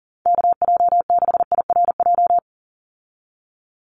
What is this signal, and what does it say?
Morse code, heard as a keyed tone, about 38 wpm, 700 Hz. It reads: KJ6IRJ
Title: USA Call Signs 40wpm